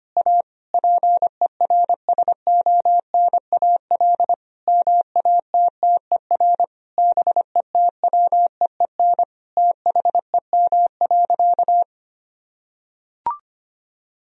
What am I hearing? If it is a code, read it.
Morse code, heard as a keyed tone, about 25 wpm, 700 Hz. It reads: A PERSONAL MATTER BETWEEN THEM.
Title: Sentences from Top 1000 Words - Review Part 4 - 25wpm